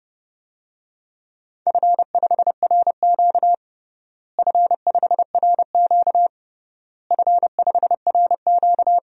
Morse code, heard as a keyed tone, about 30 wpm, 700 Hz. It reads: F5RQ F5RQ F5RQ
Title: DXCC-100 Call Signs - 4 Characters - Repeated 3x - 30wpm